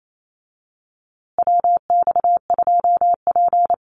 Morse code, heard as a keyed tone, about 28 wpm, 700 Hz. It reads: WX2P
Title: USA Call Signs Repeated 3x 28wpm